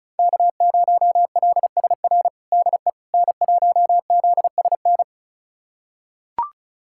Morse code, heard as a keyed tone, about 36 wpm, 700 Hz. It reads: K0LSR DE N1ZSN
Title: USA Callsign DE Callsign - 5 Characters - 35wpm